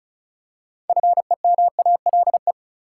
Morse code, heard as a keyed tone, about 36 wpm, 700 Hz. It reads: FEMALE
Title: Six Letter Words - 35wpm